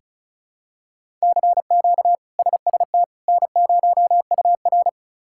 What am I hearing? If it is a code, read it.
Morse code, heard as a keyed tone, about 36 wpm, 700 Hz. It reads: CQ SST N0UR
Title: K1USN SST QSO Practice - 35wpm